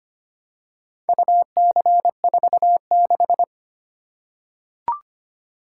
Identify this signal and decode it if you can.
Morse code, heard as a keyed tone, about 25 wpm, 700 Hz. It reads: UC46
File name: practice / 25wpm / Four Letter-Number Combinations - 25wpm